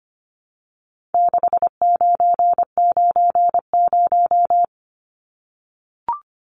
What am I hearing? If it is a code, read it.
Morse code, heard as a keyed tone, about 25 wpm, 700 Hz. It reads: 6990
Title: Numbers - 4-Digit 25wpm